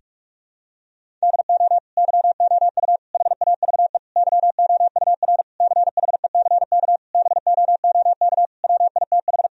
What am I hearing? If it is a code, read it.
Morse code, heard as a keyed tone, about 46 wpm, 700 Hz. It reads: DO YOU HAVE YOUR CHECK BOOK WITH
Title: Sentences from Top 1000 Words - Review Part 3 - 45wpm